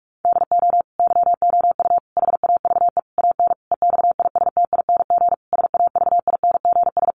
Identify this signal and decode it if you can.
Morse code, heard as a keyed tone, about 44 wpm, 700 Hz. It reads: DO YOU HAVE AN EXISTING SAVINGS
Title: Sentences from Top 1500 Words 45wpm